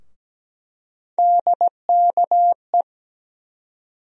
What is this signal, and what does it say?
Morse code, heard as a keyed tone, about 17 wpm, 700 Hz. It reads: DKE
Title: USA Call Sign Postfixes - 3 Letters 17wpm